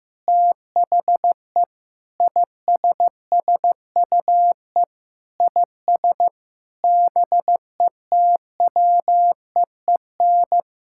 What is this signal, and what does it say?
Morse code, heard as a keyed tone, about 15 wpm, 700 Hz. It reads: THE ISSUE IS BETWEEN
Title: Sentences from Top 1000 Words - Review Part 6 - 15wpm